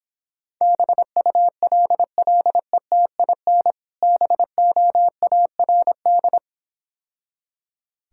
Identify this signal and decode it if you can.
Morse code, heard as a keyed tone, about 26 wpm, 700 Hz. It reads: BULLETIN BOARD